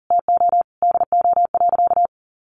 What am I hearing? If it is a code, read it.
Morse code, heard as a keyed tone, about 40 wpm, 700 Hz. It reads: TO DO.
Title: Sentences from Top 1400 Words 40wpm